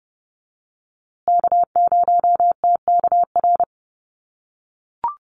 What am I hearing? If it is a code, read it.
Morse code, heard as a keyed tone, about 30 wpm, 700 Hz. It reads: K0TKR
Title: USA Call Signs Repeated 2x 30wpm